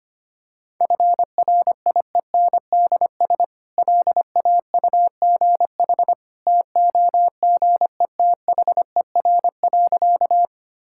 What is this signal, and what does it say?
Morse code, heard as a keyed tone, about 25 wpm, 690 Hz. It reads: FRIENDS LAUGH TOGETHER.
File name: Easy Sentences with Hints - 25wpm